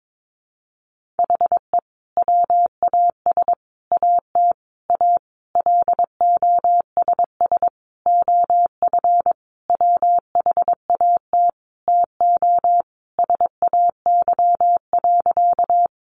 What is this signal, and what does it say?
Morse code, heard as a keyed tone, about 22 wpm, 700 Hz. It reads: HE WAS AT A LOSS OF WHAT TO SAY.
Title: Sentences from Top 1400 Words 22wpm